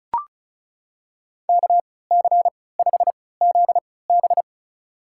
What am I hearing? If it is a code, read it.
Morse code, heard as a keyed tone, about 34 wpm, spaced as for 20 wpm, 700 Hz. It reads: KC5ZB
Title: USA Call Signs - 5 Characters - 3x Character Spacing - 35wpm